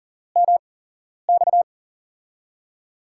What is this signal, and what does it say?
Morse code, heard as a keyed tone, about 40 wpm, spaced as for 11 wpm, 700 Hz. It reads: MX
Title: USA Call Signs - 6 Characters - 8x Character Spacing - 40wpm